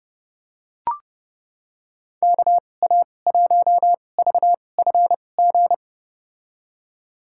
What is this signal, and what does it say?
Morse code, heard as a keyed tone, about 32 wpm, spaced as for 22 wpm, 700 Hz. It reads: KA1VFG
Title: USA Call Signs - 6 Characters - 2x Character Spacing - 30wpm